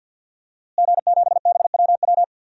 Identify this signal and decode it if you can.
Morse code, heard as a keyed tone, about 51 wpm, 700 Hz. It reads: G7BWW
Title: DXCC-100 Call Signs - 5 Characters - Repeated 2x - 50wpm